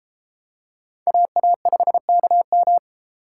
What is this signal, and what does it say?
Morse code, heard as a keyed tone, about 33 wpm, 700 Hz. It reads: AA5KM